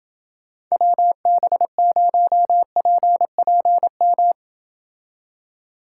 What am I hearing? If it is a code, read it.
Morse code, heard as a keyed tone, about 28 wpm, 700 Hz. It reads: WB0PPM